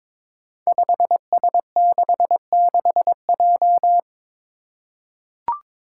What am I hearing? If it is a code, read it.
Morse code, heard as a keyed tone, about 23 wpm, 700 Hz. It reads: 5S66J
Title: Five Challenging Letter-Number Combinations 22wpm